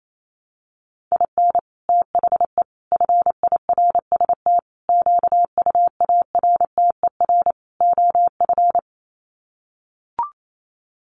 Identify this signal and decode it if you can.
Morse code, heard as a keyed tone, about 28 wpm, 700 Hz. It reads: IN THE FIRST QUARTER OF
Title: Sets of 5 Words Encore2 28wpm